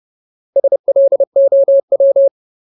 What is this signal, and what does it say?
Morse code, heard as a keyed tone, about 30 wpm, 550 Hz. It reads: SLOW
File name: Ultimate Word List / 30WPM Ultimate